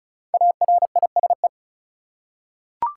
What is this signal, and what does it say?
Morse code, heard as a keyed tone, about 34 wpm, 700 Hz. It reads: ARISE
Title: Five Letter Words - 35wpm